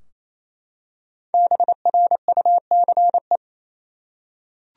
Morse code, heard as a keyed tone, about 28 wpm, 700 Hz. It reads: BRUCE
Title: USA Common Names - Max 5 Letters - 28wpm